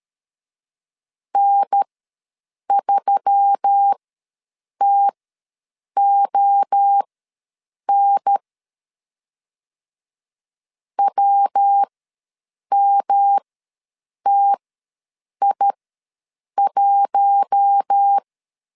Morse code, heard as a keyed tone, about 13 wpm, spaced as for 7 wpm, 800 Hz. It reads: N3TON WMTI1
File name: LEZIONI / 5 LBVC / 5 L ws 12wpm 2958